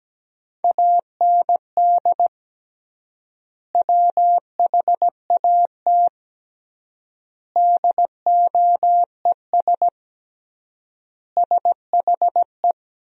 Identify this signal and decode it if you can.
Morse code, heard as a keyed tone, about 17 wpm, 700 Hz. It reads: AND WHAT DOES SHE
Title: 3x Word-Spacing - Sentences from Top 400 Words - 17wpm